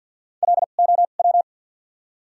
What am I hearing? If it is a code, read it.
Morse code, heard as a keyed tone, about 53 wpm, spaced as for 37 wpm, 700 Hz. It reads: LOW
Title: 2x Character Spacing - Top 300 Words - 50wpm